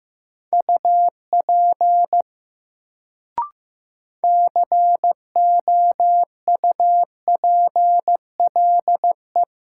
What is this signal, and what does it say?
Morse code, heard as a keyed tone, about 15 wpm, 700 Hz. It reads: UP COUPLE
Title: Sets of 2 Words - New Word Drill - 15wpm